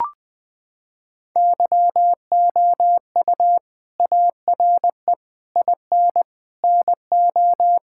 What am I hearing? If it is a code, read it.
Morse code, heard as a keyed tone, about 20 wpm, 700 Hz. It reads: YOU ARE IN NO
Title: Sentences from Top 1600 Words 20wpm